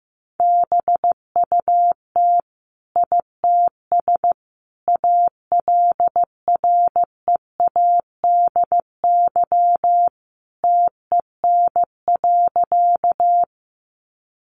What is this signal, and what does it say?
Morse code, heard as a keyed tone, about 15 wpm, 700 Hz. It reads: BUT ITS ALREADY TEN.
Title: Sentences from Top 700 Words 15wpm